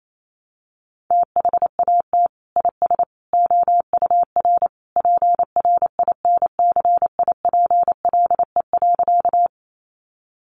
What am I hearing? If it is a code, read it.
Morse code, heard as a keyed tone, about 28 wpm, 700 Hz. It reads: THAT IS OUR PRINCIPLE.
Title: Sentences from Top 1400 Words 28wpm